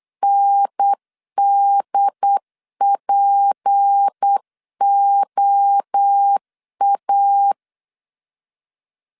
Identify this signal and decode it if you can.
Morse code, heard as a keyed tone, about 8 wpm, 800 Hz. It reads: NDPOA
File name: LEZIONI / 6 PKX0 / 6 P 8wpm 3025